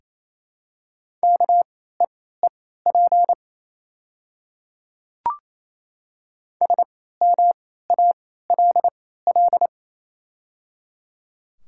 Morse code, heard as a keyed tone, about 28 wpm, spaced as for 16 wpm, 700 Hz. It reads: KEEP SMALL
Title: 3x Character Spacing - Top 300 Words - 28wpm